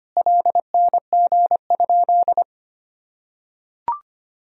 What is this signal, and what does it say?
Morse code, heard as a keyed tone, about 25 wpm, 700 Hz. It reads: LNG?
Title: Sentences from Top 500 Words and Abbreviations 25wpm